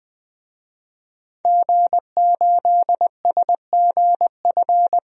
Morse code, heard as a keyed tone, about 20 wpm, 700 Hz. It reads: G8SGF
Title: DXCC-100 Call Signs - 5 Characters - 20wpm